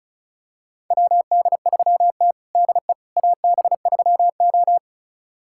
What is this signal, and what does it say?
Morse code, heard as a keyed tone, about 36 wpm, 700 Hz. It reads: WD3T DE AB3O
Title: USA Callsign DE Callsign - 4 Characters - 35wpm